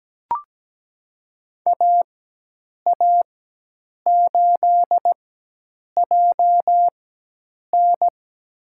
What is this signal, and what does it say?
Morse code, heard as a keyed tone, about 17 wpm, spaced as for 8 wpm, 700 Hz. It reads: AA8JN
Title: USA Call Signs - 5 Characters - 4x Character Spacing - 17wpm